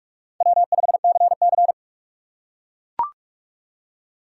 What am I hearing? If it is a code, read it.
Morse code, heard as a keyed tone, about 44 wpm, 700 Hz. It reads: W5CC
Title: USA Call Signs - 2x Character Spacing - 45wpm